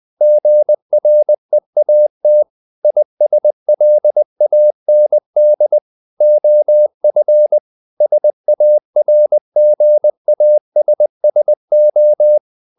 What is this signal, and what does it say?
Morse code, heard as a keyed tone, about 20 wpm, 600 Hz. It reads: GREAT ISLAND OF SARGASSO
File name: the old man and the sea part 2 20wpm